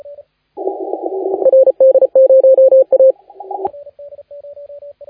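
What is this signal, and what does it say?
Morse code, heard as a keyed tone, about 35 wpm, 530 Hz, with radio noise. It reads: RD0A
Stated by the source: Morse contest audio recorded by LA4C during the SAC CW 2018 contest.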